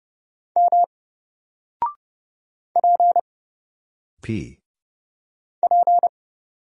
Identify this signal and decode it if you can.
Morse code, heard as a keyed tone, about 30 wpm, 700 Hz. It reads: M P P
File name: Single-Letter-Number-30wpm